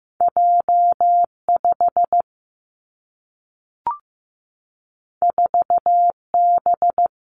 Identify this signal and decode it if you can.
Morse code, heard as a keyed tone, about 15 wpm, 700 Hz. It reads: J 5 4 B
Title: Two Challenging Letter-Number Combinations 15wpm